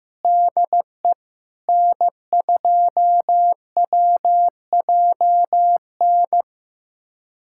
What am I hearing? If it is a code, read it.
Morse code, heard as a keyed tone, about 15 wpm, 700 Hz. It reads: DE N2WJN
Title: USA Callsign DE Callsign - 5 Characters - 15wpm